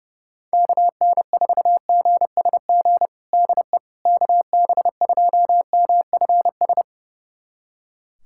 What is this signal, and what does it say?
Morse code, heard as a keyed tone, about 30 wpm, 700 Hz. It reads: KN4GSG DE KB2MFS
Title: USA Callsign DE Callsign - 6 Characters - 30wpm